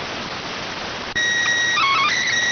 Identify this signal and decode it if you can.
SSTV (slow-scan television), PD 290.